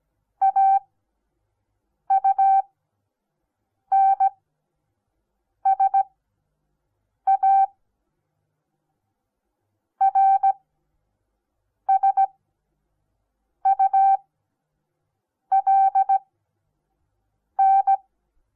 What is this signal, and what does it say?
Morse code, heard as a keyed tone, about 17 wpm, spaced as for 6 wpm, 780 Hz. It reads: AUNSA RSULN
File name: Onlinelehrgang / morsen / DJ4UF Morsen08